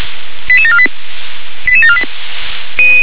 Five-tone selective call (ZVEI), heard as quick runs of five tones, tones of about 70 ms. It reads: 88748 88748